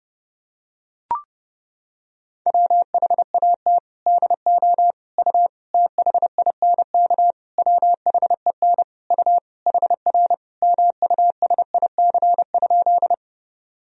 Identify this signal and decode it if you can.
Morse code, heard as a keyed tone, about 32 wpm, 700 Hz. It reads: WHAT DO U THINK WHEN U HR MUSIC?
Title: Sentences from Top 300 Words and Abbreviations 30wpm